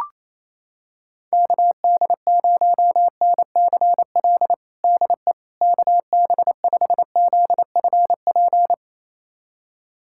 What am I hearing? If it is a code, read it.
Morse code, heard as a keyed tone, about 28 wpm, 700 Hz. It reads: KD0NCL DE KB5ZFP
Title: USA Callsign DE Callsign - 6 Characters - 28wpm